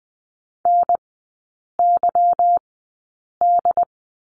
Morse code, heard as a keyed tone, about 20 wpm, spaced as for 8 wpm, 690 Hz. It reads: NYD
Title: 205 Lesson 33e - Letter j - Three Characters 20wpm